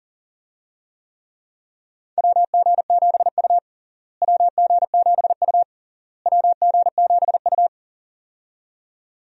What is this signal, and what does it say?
Morse code, heard as a keyed tone, about 37 wpm, 700 Hz. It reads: WG7U WG7U WG7U
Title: USA Call Signs - 4 Characters - Repeated 3x - 40wpm